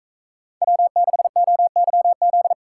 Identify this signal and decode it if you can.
Morse code, heard as a keyed tone, about 43 wpm, 700 Hz. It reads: W6OYZ